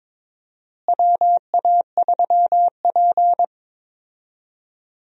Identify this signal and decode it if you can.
Morse code, heard as a keyed tone, about 22 wpm, 700 Hz. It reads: WA3P
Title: USA Call Signs - 4 Characters - Repeated 3x - 22wpm